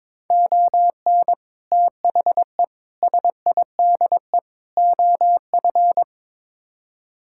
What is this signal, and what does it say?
Morse code, heard as a keyed tone, about 22 wpm, 700 Hz. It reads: ON THE SIDE OF
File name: practice / 22wpm / Build A Phrase - 5-Word Phrases - 22wpm